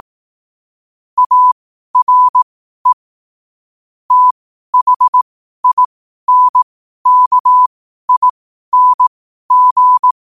Morse code, heard as a keyed tone, about 18 wpm, spaced as for 12 wpm, 1000 Hz. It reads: ARE THINKING
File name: qotdfiles / 12wpm